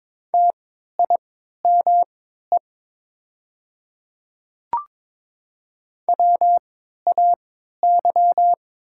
Morse code, heard as a keyed tone, about 22 wpm, spaced as for 12 wpm, 700 Hz. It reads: TIME WAY
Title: 3x Character Spacing - Top 100 Words - 22wpm